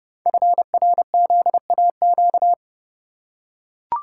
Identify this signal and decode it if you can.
Morse code, heard as a keyed tone, about 30 wpm, 700 Hz. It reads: FRZAQ
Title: Code Groups - 5 Letters - 30wpm